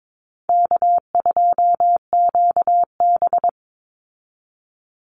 Morse code, heard as a keyed tone, about 22 wpm, 700 Hz. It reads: K2QB
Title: Call Signs Repeated 2x - 22wpm